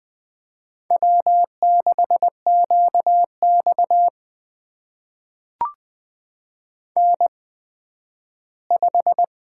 Morse code, heard as a keyed tone, about 20 wpm, spaced as for 12 wpm, 700 Hz. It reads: W6QX N5
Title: USA Call Signs - 4 Characters - 8x Character Spacing - 20wpm